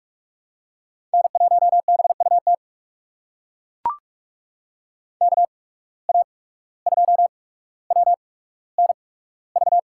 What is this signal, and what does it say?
Morse code, heard as a keyed tone, about 42 wpm, spaced as for 18 wpm, 700 Hz. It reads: N1BUT KA2WNV
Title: USA Call Signs - 8x Character Spacing - 45wpm